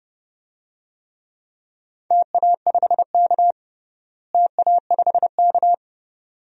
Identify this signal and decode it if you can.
Morse code, heard as a keyed tone, about 30 wpm, 700 Hz. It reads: TA5K TA5K
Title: DXCC-100 Call Signs - 4 Characters - Repeated 2x - 30wpm